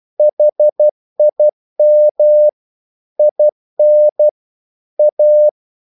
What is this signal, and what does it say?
Morse code, heard as a keyed tone, about 12 wpm, 600 Hz. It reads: HIM IN A
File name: mp3 files / the gardener - 12wpm